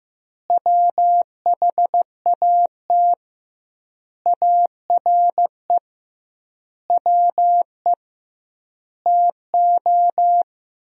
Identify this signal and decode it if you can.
Morse code, heard as a keyed tone, about 15 wpm, 700 Hz. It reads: WHAT ARE WE TO
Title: 2x Word-Spacing - Sentences from Top 100 Words - 15wpm